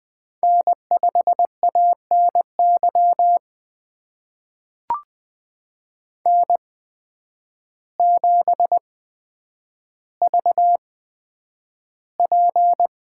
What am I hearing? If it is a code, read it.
Morse code, heard as a keyed tone, about 20 wpm, spaced as for 9 wpm, 700 Hz. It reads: N5ANY N7VP
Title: USA Call Signs - 5 Characters - 8x Character Spacing - 20wpm